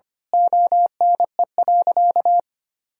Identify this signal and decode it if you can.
Morse code, heard as a keyed tone, about 25 wpm, 700 Hz. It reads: ONE.
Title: Sentences from Top 1000 Words - Review Part 6 - 25wpm